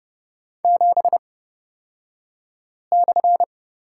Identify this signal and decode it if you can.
Morse code, heard as a keyed tone, about 30 wpm, spaced as for 5 wpm, 700 Hz. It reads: 7/
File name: Lesson 38d - Letter q - Two Characters 30wpm